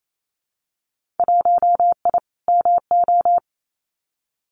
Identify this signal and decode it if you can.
Morse code, heard as a keyed tone, about 28 wpm, 700 Hz. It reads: 1I MO
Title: Winter Field Day Exchanges Encore 28wpm